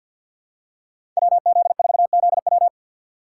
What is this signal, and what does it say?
Morse code, heard as a keyed tone, about 52 wpm, 700 Hz. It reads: WZ4ZW
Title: USA Call Signs - 4x Character Spacing - 50wpm